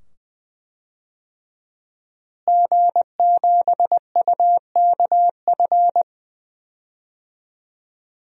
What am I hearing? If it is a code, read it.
Morse code, heard as a keyed tone, about 20 wpm, 700 Hz. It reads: G7UKF
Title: DXCC-100 Call Signs - 5 Characters - Repeated 2x - 20wpm